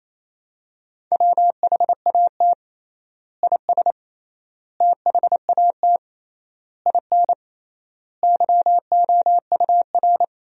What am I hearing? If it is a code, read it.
Morse code, heard as a keyed tone, about 28 wpm, 700 Hz. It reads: WHAT IS THAT IN YOUR
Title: 3x Word-Spacing - Sentences from Top 200 Words - 28wpm